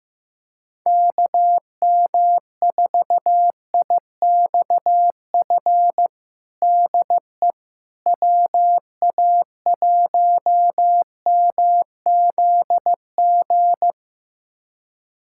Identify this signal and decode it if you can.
Morse code, heard as a keyed tone, about 16 wpm, 700 Hz. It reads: KM4IXF DE WA1MZG
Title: USA Callsign DE Callsign 15wpm